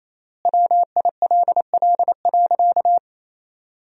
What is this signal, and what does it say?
Morse code, heard as a keyed tone, about 28 wpm, 700 Hz. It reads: WILL.
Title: 3x Word-Spacing - Sentences from Top 200 Words - 28wpm